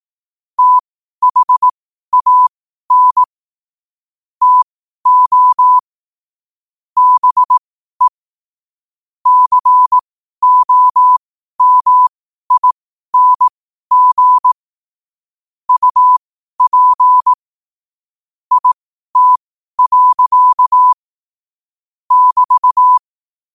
Morse code, heard as a keyed tone, about 18 wpm, spaced as for 12 wpm, 1000 Hz. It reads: THAN TO BE COMING UP IT. <BT>
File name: qotdfiles / 12wpm